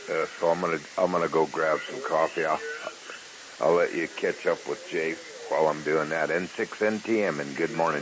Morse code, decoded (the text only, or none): none